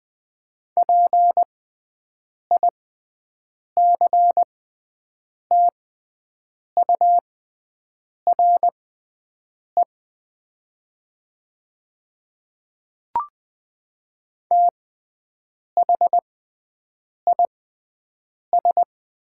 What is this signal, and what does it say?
Morse code, heard as a keyed tone, about 20 wpm, spaced as for 7 wpm, 700 Hz. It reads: PICTURE THIS
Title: 6x Character Spacing - Top 200 Words - 20wpm